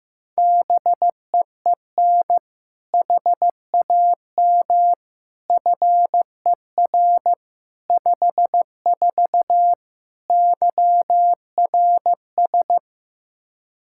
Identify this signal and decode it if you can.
Morse code, heard as a keyed tone, about 15 wpm, 700 Hz. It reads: BEEN HAM FER 54 YRS
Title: K1USN SST QSO Practice 15wpm